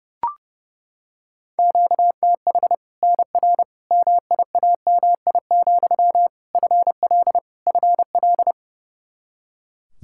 Morse code, heard as a keyed tone, about 30 wpm, 700 Hz. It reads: QTH NR MIAMI, FL FL
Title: Practice QSO - Exchange 1 - Beginning - 30wpm